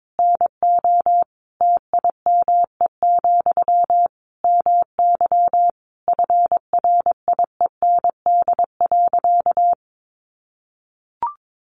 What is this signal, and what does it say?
Morse code, heard as a keyed tone, about 22 wpm, 700 Hz. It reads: NO TIME, MY FRIEND.
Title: Sentences from Top 1100 Words 22wpm